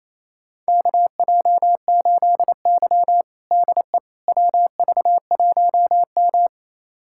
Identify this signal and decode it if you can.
Morse code, heard as a keyed tone, about 28 wpm, 700 Hz. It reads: KJ8Y DE WV1M
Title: USA Callsign DE Callsign - 4 Characters - 28wpm